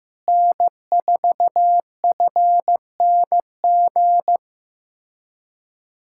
Morse code, heard as a keyed tone, about 15 wpm, 700 Hz. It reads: N4FNG
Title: USA Callsign DE Callsign - 5 Characters - 15wpm